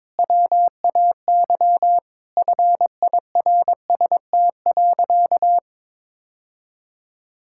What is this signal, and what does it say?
Morse code, heard as a keyed tone, about 22 wpm, 700 Hz. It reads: WAY FIRST.
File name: Sentences with 7-Word Phrases - 22wpm